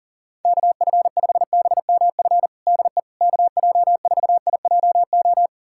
Morse code, heard as a keyed tone, about 40 wpm, 690 Hz. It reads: KF5BMF DE KJ4IJO
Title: USA Callsign DE Callsign - 6 Characters - 40wpm